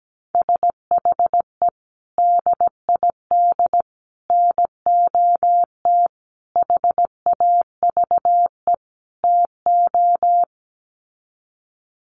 Morse code, heard as a keyed tone, about 17 wpm, 700 Hz. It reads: SHE DID NOT HAVE TO
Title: Sets of 5 Words Encore2 17wpm